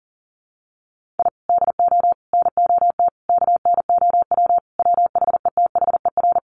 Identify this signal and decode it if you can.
Morse code, heard as a keyed tone, about 40 wpm, 700 Hz. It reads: I DO NOT KNOW WHETHER
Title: Sentences from Top 1500 Words 40wpm